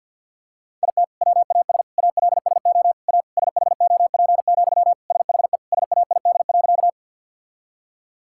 Morse code, heard as a keyed tone, about 50 wpm, 700 Hz. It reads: IT WAS ALSO A SHOP, SHE SAID.
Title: Sentences from Top 1600 Words 50wpm